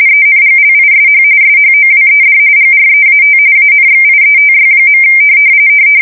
RTTY, about 45.45 baud. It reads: ENGLISH SHOWED GREAT ANXIETY.
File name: fsk170 / USS-New-Jersey